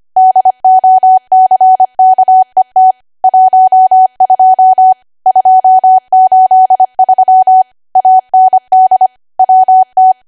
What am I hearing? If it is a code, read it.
Morse code, heard as a keyed tone, about 25 wpm, 750 Hz. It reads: DOCKET 12 283 AND WT
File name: cwa / W1AW-Level3 / 25WPM